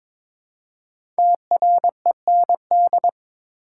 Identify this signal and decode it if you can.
Morse code, heard as a keyed tone, about 22 wpm, 700 Hz. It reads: TREND
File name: 5 Letter Words - First 3 Letters Given - 22wpm